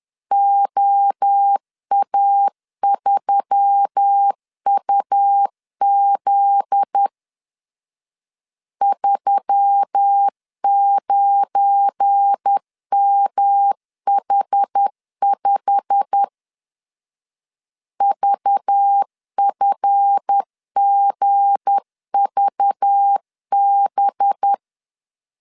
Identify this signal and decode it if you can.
Morse code, heard as a keyed tone, about 10 wpm, 800 Hz. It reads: OA3UZ 39MH5 VFGVB
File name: LEZIONI / 8 YJ45 / 8 YJ45 10wpm 2549